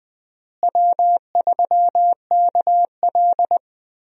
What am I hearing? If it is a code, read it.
Morse code, heard as a keyed tone, about 20 wpm, 700 Hz. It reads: W3KL
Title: USA Call Signs - 4 Characters - Repeated 3x - 20wpm